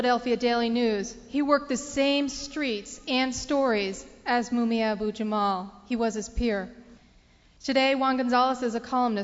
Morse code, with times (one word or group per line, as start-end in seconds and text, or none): none